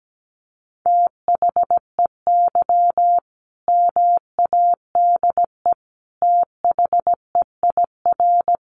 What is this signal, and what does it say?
Morse code, heard as a keyed tone, about 17 wpm, 700 Hz. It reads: THEY MADE THEIR
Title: Sentences from Top 1400 Words 17wpm